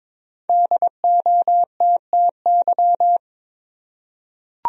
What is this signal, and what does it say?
Morse code, heard as a keyed tone, about 23 wpm, 700 Hz. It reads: DOTTY
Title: USA Common Names - Max 5 Letters - 22wpm